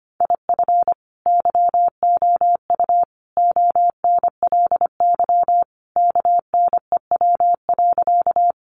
Morse code, heard as a keyed tone, about 25 wpm, 700 Hz. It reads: IF YOU ONLY KNEW.
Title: Sentences from Top 400 Words 25wpm